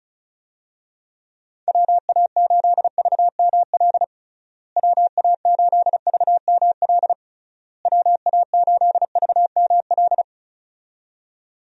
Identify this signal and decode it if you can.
Morse code, heard as a keyed tone, about 36 wpm, 700 Hz. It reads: WA8VML WA8VML WA8VML
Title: USA Call Signs - 6 Characters - Repeated 3x - 35wpm